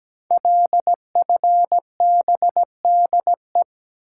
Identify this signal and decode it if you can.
Morse code, heard as a keyed tone, about 17 wpm, 700 Hz. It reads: LFBDE
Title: Code Groups - 5 Letters - 17wpm